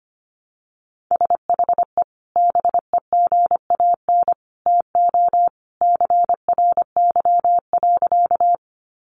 Morse code, heard as a keyed tone, about 25 wpm, 700 Hz. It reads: SHE BEGAN TO CRY.
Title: Sentences from Top 500 Words 25wpm